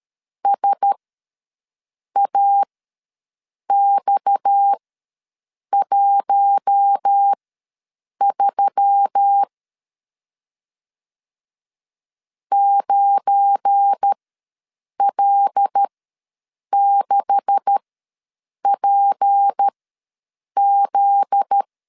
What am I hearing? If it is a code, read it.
Morse code, heard as a keyed tone, about 13 wpm, spaced as for 7 wpm, 800 Hz. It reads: SAX13 9L6PZ
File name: LEZIONI / 9 286 ws 12wpm 3351